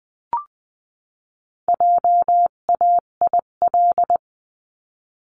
Morse code, heard as a keyed tone, about 20 wpm, 700 Hz. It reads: JAIL
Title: 258 Lesson 42c - Words 16-20wpm